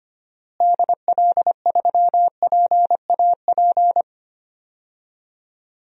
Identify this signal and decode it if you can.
Morse code, heard as a keyed tone, about 25 wpm, 700 Hz. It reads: DL3PAP
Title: DXCC-100 Call Signs - 6 Characters - Repeated 2x - 25wpm